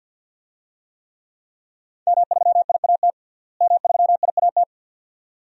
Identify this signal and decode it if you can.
Morse code, heard as a keyed tone, about 50 wpm, 700 Hz. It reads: M3IAT M3IAT
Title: DXCC-100 Call Signs - 5 Characters - Repeated 2x - 50wpm